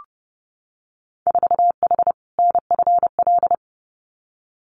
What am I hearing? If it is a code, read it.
Morse code, heard as a keyed tone, about 30 wpm, 700 Hz. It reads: 4H NFL
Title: Winter Field Day Exchanges Encore 30wpm